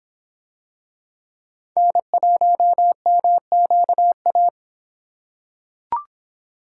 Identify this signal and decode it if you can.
Morse code, heard as a keyed tone, about 23 wpm, 700 Hz. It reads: N1MQA